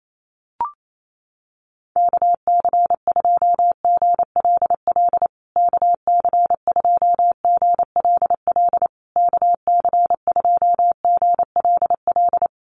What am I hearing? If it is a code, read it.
Morse code, heard as a keyed tone, about 28 wpm, 700 Hz. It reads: KC2GLL KC2GLL KC2GLL
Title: USA Call Signs Repeated 3x 28wpm